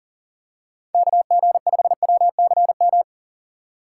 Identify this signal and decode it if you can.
Morse code, heard as a keyed tone, about 40 wpm, 700 Hz. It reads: KG5WCM
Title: USA Call Signs - 6 Characters - 2x Character Spacing - 40wpm